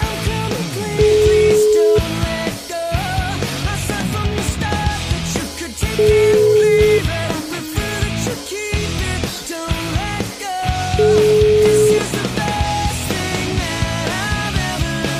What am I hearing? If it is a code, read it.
Ringback tone (Europe, 425 Hz), with noise.